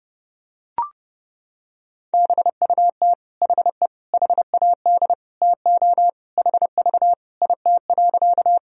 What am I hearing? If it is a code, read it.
Morse code, heard as a keyed tone, about 30 wpm, 700 Hz. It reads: BUT HE HAD TO HV IT.
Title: Sentences from Top 100 Words and Abbreviations 30wpm